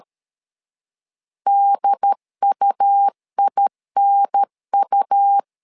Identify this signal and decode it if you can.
Morse code, heard as a keyed tone, about 13 wpm, 800 Hz. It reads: DUINU
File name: LEZIONI / 3 MWRO / 3 M 12wpm 1541